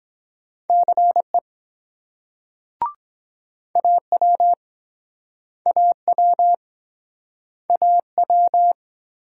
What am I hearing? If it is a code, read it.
Morse code, heard as a keyed tone, about 24 wpm, 700 Hz. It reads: CE AW AW AW